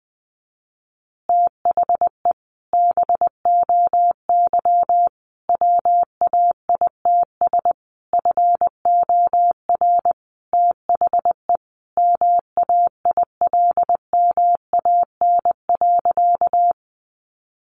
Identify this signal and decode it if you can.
Morse code, heard as a keyed tone, about 20 wpm, 700 Hz. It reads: THE BOY WAITS FOR THE MAILMAN.
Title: Additional Easy Phrases 20wpm